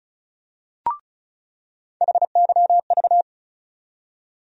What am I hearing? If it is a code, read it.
Morse code, heard as a keyed tone, about 34 wpm, 700 Hz. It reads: HYV
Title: USA Call Sign Postfixes - 3 Letters 35wpm